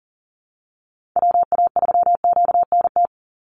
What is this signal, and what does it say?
Morse code, heard as a keyed tone, about 40 wpm, 700 Hz. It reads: WA3QNT
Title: Call Signs Repeated 3x - 40wpm